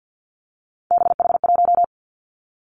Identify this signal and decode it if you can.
Morse code, heard as a keyed tone, about 50 wpm, 700 Hz. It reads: BH1
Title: Three Challenging Letter-Number Combinations 50wpm